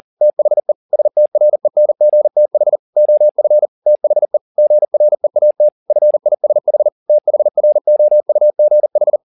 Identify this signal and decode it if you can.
Morse code, heard as a keyed tone, about 40 wpm, 600 Hz. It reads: THE STRENGTH OF THE GREAT FISH THROUGH